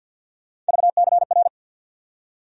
Morse code, heard as a keyed tone, about 53 wpm, 700 Hz. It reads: VCR
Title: USA Call Sign Postfixes - 3 Letters 50wpm